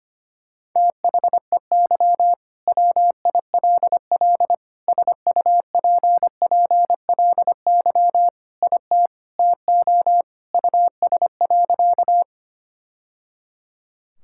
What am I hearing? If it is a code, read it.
Morse code, heard as a keyed tone, about 25 wpm, 700 Hz. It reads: THEY WILL SUPPLY IT TO US.
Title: Sentences from Top 1000 Words - Review Part 5 - 25wpm